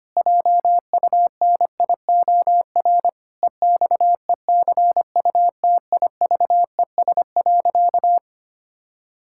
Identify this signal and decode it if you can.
Morse code, heard as a keyed tone, about 25 wpm, 700 Hz. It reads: JUNIOR EXECUTIVES.
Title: Pangram Challenge v2 - 25wpm